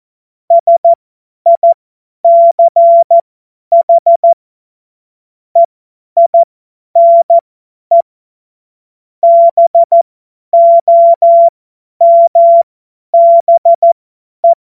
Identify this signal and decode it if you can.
Morse code, heard as a keyed tone, about 14 wpm, spaced as for 10 wpm, 680 Hz. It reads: SICH EINE BOMBE